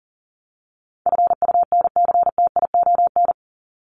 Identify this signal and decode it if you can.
Morse code, heard as a keyed tone, about 40 wpm, 700 Hz. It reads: FUNCTION
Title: Top 1001-1250 Words 40wpm